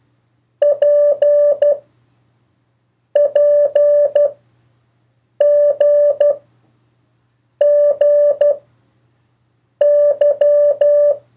Morse code, heard as a keyed tone, about 12 wpm, 580 Hz. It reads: P P G G Y
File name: 1of4-cw-letters